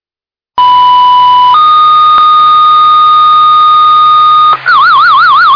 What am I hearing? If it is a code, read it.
Two-tone page, heard as a one-second tone, then a three-second tone, about 990 then 1230 Hz. It is heard from 0.6 to 4.6 s.